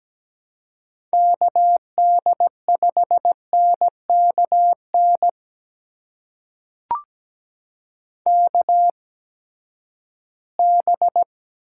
Morse code, heard as a keyed tone, about 17 wpm, spaced as for 12 wpm, 700 Hz. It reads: KD5NKN KB
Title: USA Call Signs - 6 Characters - 8x Character Spacing - 17wpm